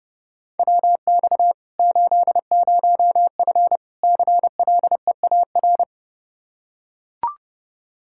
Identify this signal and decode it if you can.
Morse code, heard as a keyed tone, about 32 wpm, 700 Hz. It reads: WX 80F CLEAR
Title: Practice QSO - Exchange 2 - Info - 30wpm